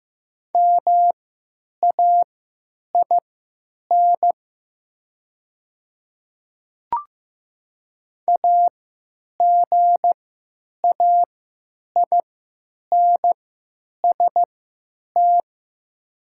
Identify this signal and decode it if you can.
Morse code, heard as a keyed tone, about 15 wpm, spaced as for 8 wpm, 700 Hz. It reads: MAIN AGAINST
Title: 3x Character Spacing - Top 500 Words - 15wpm